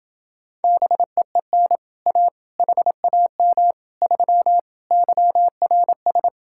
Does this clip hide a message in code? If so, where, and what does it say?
Yes: Morse code from 0.6 to 6.3 s, heard as a keyed tone, about 28 wpm, 700 Hz. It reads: BEEN A HAM 3 YRS